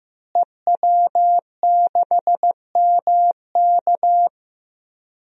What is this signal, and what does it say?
Morse code, heard as a keyed tone, about 15 wpm, 700 Hz. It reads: EW6MK
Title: DXCC-100 Call Signs - 5 Characters - Repeated 3x - 15wpm